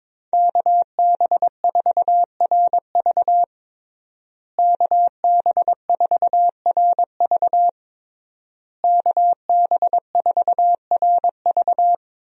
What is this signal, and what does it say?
Morse code, heard as a keyed tone, about 22 wpm, 700 Hz. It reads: KB4RV KB4RV KB4RV
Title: USA Call Signs - 5 Characters - Repeated 3x - 22wpm